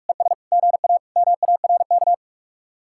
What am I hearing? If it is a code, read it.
Morse code, heard as a keyed tone, about 44 wpm, 700 Hz. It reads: ES GA MARK